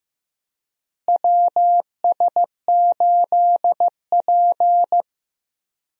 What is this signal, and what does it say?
Morse code, heard as a keyed tone, about 15 wpm, 700 Hz. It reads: WS8P
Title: USA Call Signs - 4 Characters - Repeated 2x - 15wpm